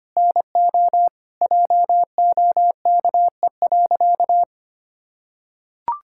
Morse code, heard as a keyed tone, about 25 wpm, 700 Hz. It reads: NO JOKE.
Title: Pangram Challenge v2 - 25wpm